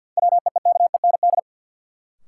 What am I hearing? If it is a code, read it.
Morse code, heard as a keyed tone, about 52 wpm, 700 Hz. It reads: WEEKEND
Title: Sets of 5 Words - New Word Drill - 50wpm